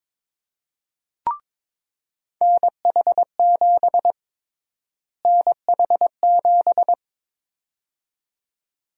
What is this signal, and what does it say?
Morse code, heard as a keyed tone, about 22 wpm, 700 Hz. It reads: NH7 NH7
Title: USA Call Sign Prefixes - Repeated 2x - 22wpm